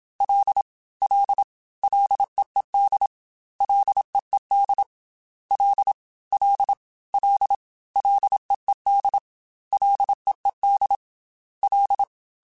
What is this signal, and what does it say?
Morse code, heard as a keyed tone, about 27 wpm, 800 Hz. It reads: L L LEED LEED L L L LEED LEED L
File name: Ls / Single L m3s / Single L Prefixe 3x2 Twice @25